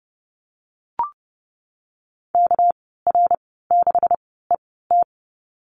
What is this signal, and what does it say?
Morse code, heard as a keyed tone, about 30 wpm, spaced as for 17 wpm, 700 Hz. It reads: KR6ET
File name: Lesson 37g - Letter x - Call Signs 30wpm